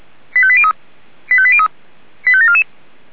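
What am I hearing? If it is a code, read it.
Five-tone selective call (ZVEI), heard as quick runs of five tones, tones of about 70 ms. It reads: 85803 85803 85744
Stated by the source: ZVEI five-tone selective call 85827, received by SDR.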